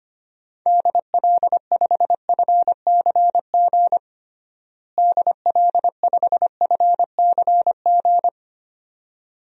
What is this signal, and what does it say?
Morse code, heard as a keyed tone, about 25 wpm, 700 Hz. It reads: DL5FCG DL5FCG
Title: DXCC-100 Call Signs - 6 Characters - Repeated 2x - 25wpm